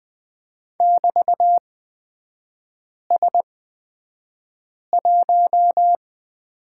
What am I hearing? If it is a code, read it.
Morse code, heard as a keyed tone, about 20 wpm, spaced as for 5 wpm, 700 Hz. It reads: <BT>S1
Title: 236 Lesson 38e - Letter q - Three Characters 20wpm